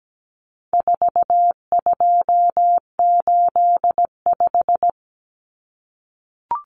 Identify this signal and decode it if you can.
Morse code, heard as a keyed tone, about 17 wpm, 700 Hz. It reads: 4285
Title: Common Cut Numbers - 4-Digit 17wpm